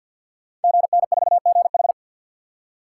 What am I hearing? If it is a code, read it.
Morse code, heard as a keyed tone, about 50 wpm, 690 Hz. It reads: GN4GH
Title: Code Groups - 5 Letter:Numbers - 50wpm